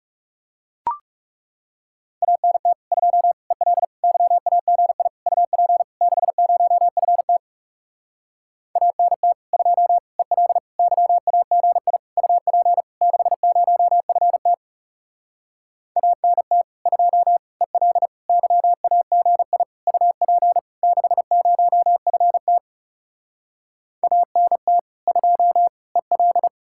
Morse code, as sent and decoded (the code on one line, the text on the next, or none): .- -. - / ..--- / . .-.. / -.-- .- --. .. / ..- .--. / -.... ----- ..-. - / .- -. - / ..--- / . .-.. / -.-- .- --. .. / ..- .--. / -.... ----- ..-. - / .- -. - / ..--- / . .-.. / -.-- .- --. .. / ..- .--. / -.... ----- ..-. - / .- -. - / ..--- / . .-..
ANT 2 EL YAGI UP 60FT ANT 2 EL YAGI UP 60FT ANT 2 EL YAGI UP 60FT ANT 2 EL